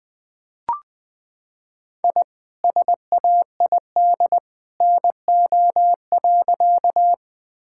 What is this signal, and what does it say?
Morse code, heard as a keyed tone, about 20 wpm, 700 Hz. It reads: I SAID NO.
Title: Sentences with 2-Word Phrases - 20wpm